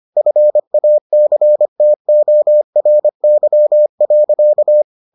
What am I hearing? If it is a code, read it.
Morse code, heard as a keyed tone, about 25 wpm, 600 Hz. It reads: FACTORY.